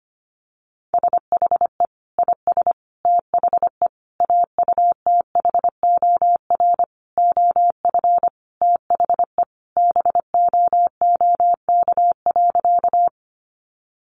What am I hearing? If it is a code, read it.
Morse code, heard as a keyed tone, about 25 wpm, 700 Hz. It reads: SHE IS THE AUTHOR OF THE BOOK.
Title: Sentences from Top 1200 Words 25wpm